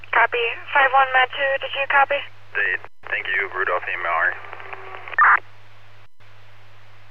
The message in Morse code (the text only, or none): none